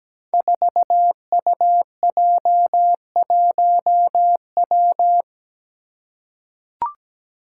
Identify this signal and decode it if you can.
Morse code, heard as a keyed tone, about 17 wpm, 700 Hz. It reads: 4UJ1W
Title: Five Challenging Letter-Number Combinations 17wpm